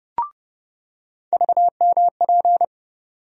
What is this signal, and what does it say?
Morse code, heard as a keyed tone, about 30 wpm, 700 Hz. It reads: VMP
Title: USA Call Sign Postfixes - 3 Letters 30wpm